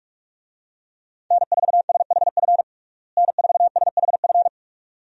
Morse code, heard as a keyed tone, about 46 wpm, 700 Hz. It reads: N4SHF N4SHF
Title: USA Call Signs - 5 Characters - Repeated 2x - 45wpm